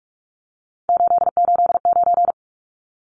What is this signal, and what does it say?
Morse code, heard as a keyed tone, about 44 wpm, 700 Hz. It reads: 889
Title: Numbers - 3-Digit 45wpm